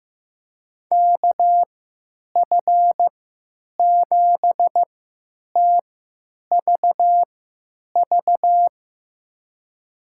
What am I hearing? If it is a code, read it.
Morse code, heard as a keyed tone, about 15 wpm, spaced as for 9 wpm, 700 Hz. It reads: KF7TVV
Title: USA Call Signs - 3x Character Spacing - 15wpm